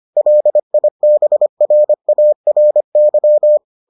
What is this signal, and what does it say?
Morse code, heard as a keyed tone, about 25 wpm, 600 Hz. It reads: LIBRARY